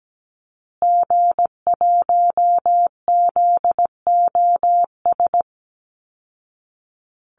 Morse code, heard as a keyed tone, about 17 wpm, 700 Hz. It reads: G1ZOS
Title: Call Signs Repeated 2x - 17wpm